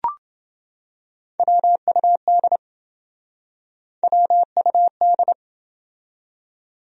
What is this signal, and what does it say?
Morse code, heard as a keyed tone, about 27 wpm, 700 Hz. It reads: WUD WUD